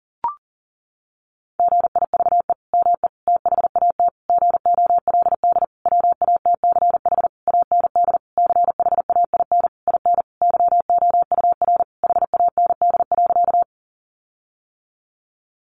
Morse code, heard as a keyed tone, about 40 wpm, 700 Hz. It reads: GIVE ME THAT GOLD WATCH AND CHAIN IN YOUR HAND.
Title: Sentences from Top 1500 Words 40wpm